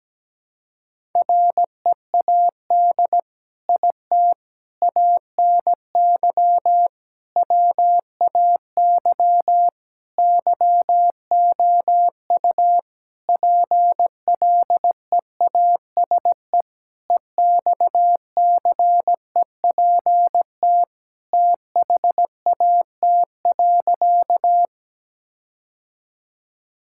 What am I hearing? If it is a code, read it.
Morse code, heard as a keyed tone, about 17 wpm, 700 Hz. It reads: READ IT ANY WAY YOU PLEASE EXCEPT THAT.
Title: Sentences from Top 1600 Words 17wpm